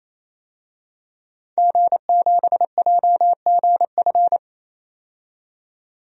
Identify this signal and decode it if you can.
Morse code, heard as a keyed tone, about 28 wpm, 700 Hz. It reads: G7JGF
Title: DXCC-100 Call Signs - 5 Characters - 28wpm